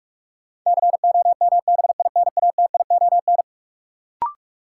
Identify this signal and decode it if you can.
Morse code, heard as a keyed tone, about 46 wpm, 700 Hz. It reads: COMBINATION
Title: Sentences from Top 1300 Words - New Word Drill Alt - 45wpm